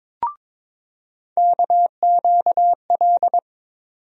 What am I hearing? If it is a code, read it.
Morse code, heard as a keyed tone, about 22 wpm, 700 Hz. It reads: KQL
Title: USA Call Sign Postfixes - 3 Letters 22wpm